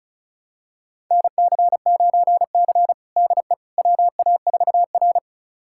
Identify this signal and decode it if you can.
Morse code, heard as a keyed tone, about 36 wpm, 700 Hz. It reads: NC9C DE WA4R
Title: USA Callsign DE Callsign - 4 Characters - 35wpm